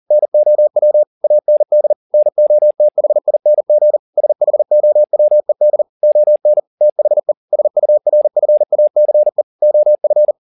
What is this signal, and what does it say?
Morse code, heard as a keyed tone, about 40 wpm, 600 Hz. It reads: NOW AND NOTHING SHOWED ON THE SURFACE OF